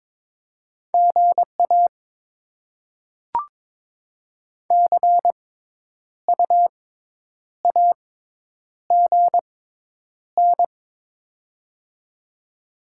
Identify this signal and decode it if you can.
Morse code, heard as a keyed tone, about 21 wpm, spaced as for 9 wpm, 700 Hz. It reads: GA CU AGN
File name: Top 20 QSO Elements - 6x Character Spacing - 22wpm